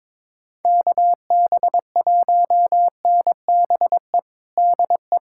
Morse code, heard as a keyed tone, about 22 wpm, 700 Hz. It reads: KB1NBE DE
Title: USA Callsign DE Callsign - 6 Characters - 22wpm